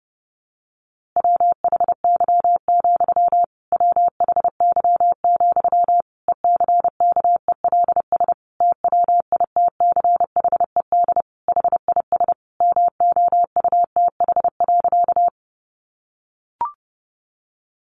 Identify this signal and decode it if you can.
Morse code, heard as a keyed tone, about 30 wpm, 700 Hz. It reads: WHY, WHY, ECKELS TWITCHED HIS MOUTH.
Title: A Sound Of Thunder 30wpm